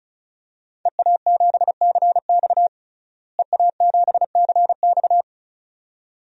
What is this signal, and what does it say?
Morse code, heard as a keyed tone, about 34 wpm, 700 Hz. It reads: EA7CX EA7CX
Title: DXCC-100 Call Signs - 5 Characters - Repeated 2x - 35wpm